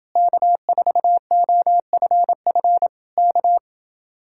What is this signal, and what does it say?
Morse code, heard as a keyed tone, about 29 wpm, 700 Hz. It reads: K4OFF K